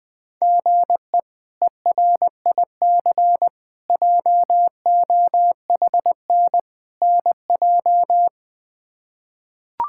Morse code, heard as a keyed tone, about 20 wpm, 700 Hz. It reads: GE ERIC JOHN NJ
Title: K1USN SST QSO Practice - 20wpm